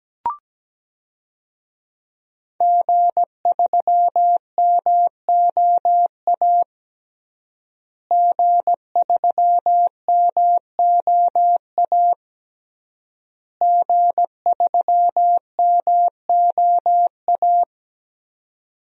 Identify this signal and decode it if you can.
Morse code, heard as a keyed tone, about 17 wpm, 700 Hz. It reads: G3MOA G3MOA G3MOA
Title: DXCC-100 Call Signs - 5 Characters - Repeated 3x - 17wpm